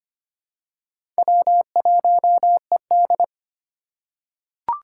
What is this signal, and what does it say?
Morse code, heard as a keyed tone, about 25 wpm, 700 Hz. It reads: W1ED
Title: USA Call Signs - 4 Characters - 2x Character Spacing - 25wpm